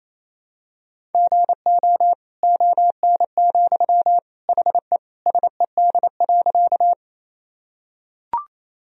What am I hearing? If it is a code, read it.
Morse code, heard as a keyed tone, about 28 wpm, 700 Hz. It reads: GO ON, HE SED.
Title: Sentences from Top 100 Words and Abbreviations 28wpm